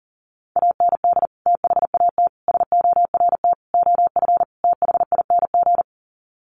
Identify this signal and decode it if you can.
Morse code, heard as a keyed tone, about 40 wpm, 700 Hz. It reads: AND THAT SORT OF THING
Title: Sets of 5 Words Encore2 40wpm